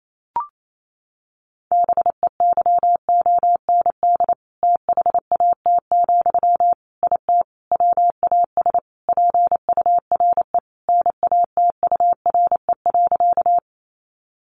Morse code, heard as a keyed tone, about 28 wpm, 700 Hz. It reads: BEYOND THAT, IT WAS PURE NATURE.
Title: Sentences from Top 1500 Words 28wpm